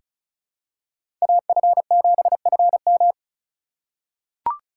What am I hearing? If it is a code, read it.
Morse code, heard as a keyed tone, about 36 wpm, 700 Hz. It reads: AF7FM
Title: USA Call Signs - 5 Characters - 4x Character Spacing - 35wpm